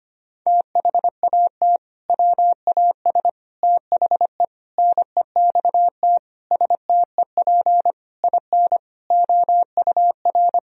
Morse code, heard as a keyed tone, about 25 wpm, 700 Hz. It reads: THAT WAS THE NEXT STEP IN OUR
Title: Sentences from Top 1000 Words - Review Part 4 - 25wpm